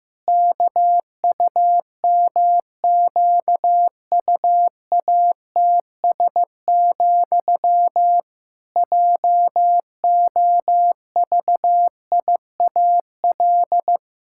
Morse code, heard as a keyed tone, about 16 wpm, 700 Hz. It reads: KUMQUATS, JOVIAL
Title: Pangram Challenge v2 - 15wpm